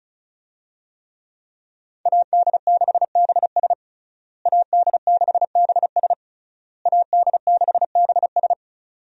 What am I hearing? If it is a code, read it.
Morse code, heard as a keyed tone, about 34 wpm, 700 Hz. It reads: AD6BS AD6BS AD6BS
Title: USA Call Signs - 5 Characters - Repeated 3x - 35wpm